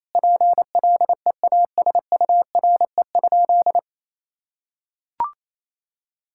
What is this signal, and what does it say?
Morse code, heard as a keyed tone, about 28 wpm, 700 Hz. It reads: PLEASURE?
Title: Sentences from Top 1600 Words 28wpm